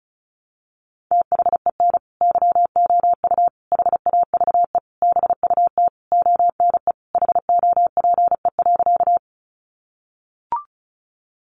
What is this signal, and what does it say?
Morse code, heard as a keyed tone, about 34 wpm, 700 Hz. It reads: THEN YOU HAVE BUT ONE HOPE.
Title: Sentences from Top 1000 Words 35wpm